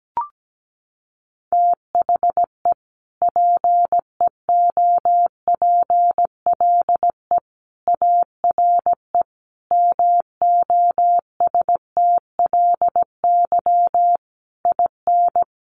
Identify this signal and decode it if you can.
Morse code, heard as a keyed tone, about 17 wpm, 700 Hz. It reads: THE PEOPLE ARE MOSTLY IN
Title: Sentences from Top 1300 Words 17wpm